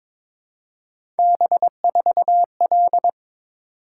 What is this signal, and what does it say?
Morse code, heard as a keyed tone, about 22 wpm, 700 Hz. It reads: B4L
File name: practice / 22wpm / Three Letter-Number Combinations - 22wpm